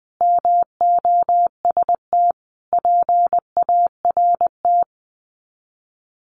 Morse code, heard as a keyed tone, about 20 wpm, 700 Hz. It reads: MOST PART
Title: Sets of 5 Words Encore 20wpm